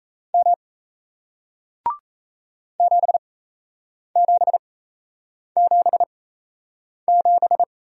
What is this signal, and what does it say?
Morse code, heard as a keyed tone, about 37 wpm, spaced as for 8 wpm, 700 Hz. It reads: M 7777